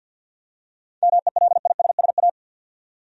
Morse code, heard as a keyed tone, about 52 wpm, 700 Hz. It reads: MELISSA
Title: USA Common Names - 50wpm